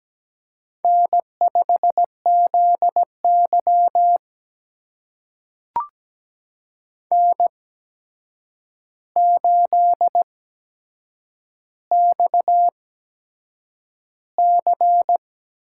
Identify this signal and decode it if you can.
Morse code, heard as a keyed tone, about 17 wpm, spaced as for 7 wpm, 700 Hz. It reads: N5ZY N8XC
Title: USA Call Signs - 4 Characters - 8x Character Spacing - 17wpm